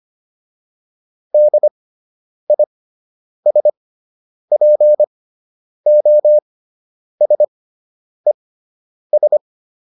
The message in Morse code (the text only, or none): DISPOSES